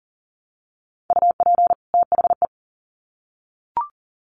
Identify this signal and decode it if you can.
Morse code, heard as a keyed tone, about 39 wpm, 700 Hz. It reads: UP THE
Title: Sets of 2 Words 40wpm